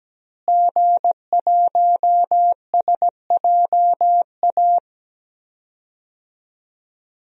Morse code, heard as a keyed tone, about 17 wpm, 700 Hz. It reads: G1SJA
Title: DXCC-100 Call Signs - 5 Characters - Repeated 2x - 17wpm